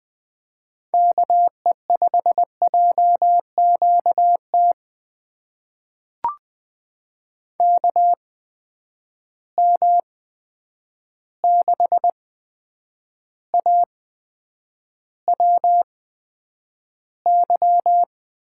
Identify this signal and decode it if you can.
Morse code, heard as a keyed tone, about 20 wpm, spaced as for 9 wpm, 700 Hz. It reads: KE5JQT KM6AWY
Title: USA Call Signs - 6 Characters - 8x Character Spacing - 20wpm